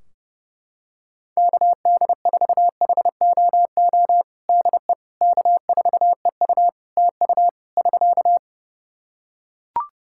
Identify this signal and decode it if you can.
Morse code, heard as a keyed tone, about 32 wpm, 700 Hz. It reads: KD4HOO DE K4EU TU <SK>
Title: Practice QSO - Exchange 4 - Ending - 30wpm